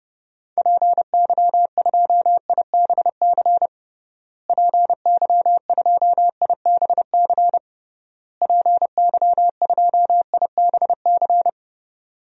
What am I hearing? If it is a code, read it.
Morse code, heard as a keyed tone, about 31 wpm, 700 Hz. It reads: PY2IBC PY2IBC PY2IBC
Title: DXCC-100 Call Signs - 6 Characters - Repeated 3x - 30wpm